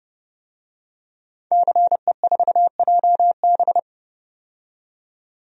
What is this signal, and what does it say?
Morse code, heard as a keyed tone, about 30 wpm, 700 Hz. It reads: CE4JB
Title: DXCC-100 Call Signs - 5 Characters - 30wpm